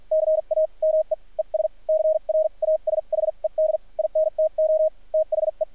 Morse code, heard as a keyed tone, about 46 wpm, spaced as for 38 wpm, 590 Hz, with radio noise. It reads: CAME ES CRASHED INTO THE
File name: archived / CRSnet